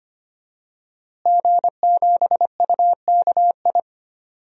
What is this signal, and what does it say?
Morse code, heard as a keyed tone, about 25 wpm, 700 Hz. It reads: G7UKI
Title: DXCC-100 Call Signs - 5 Characters - Repeated 3x - 25wpm